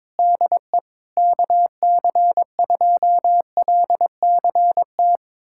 Morse code, heard as a keyed tone, about 22 wpm, 700 Hz. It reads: DE KC2LCT
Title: USA Callsign DE Callsign 22wpm